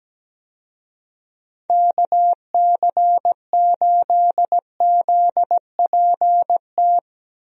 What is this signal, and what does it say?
Morse code, heard as a keyed tone, about 17 wpm, 700 Hz. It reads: KC8ZPT
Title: USA Call Signs - 6 Characters - Repeated 3x - 17wpm